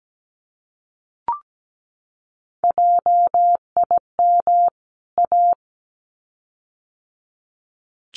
Morse code, heard as a keyed tone, about 17 wpm, 700 Hz. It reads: JIM A
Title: CWT Exchange 17wpm